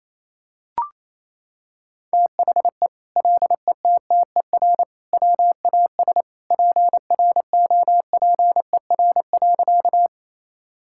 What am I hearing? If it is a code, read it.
Morse code, heard as a keyed tone, about 27 wpm, 700 Hz. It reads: THE LETTER WAS PROPER.
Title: Sentences from Top 1000 Words - Review Part 5 - 28wpm